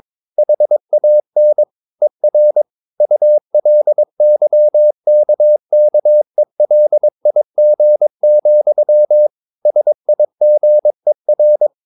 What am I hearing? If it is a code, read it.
Morse code, heard as a keyed tone, about 22 wpm, 600 Hz. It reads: HAN ER ULYKKELIG, SIGER
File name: Ferie QRS 22wpm